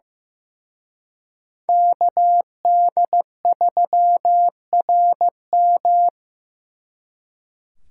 Morse code, heard as a keyed tone, about 17 wpm, 700 Hz. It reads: KD3RM
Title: Speed Racing - USA 2x2 Call Signs 15wpm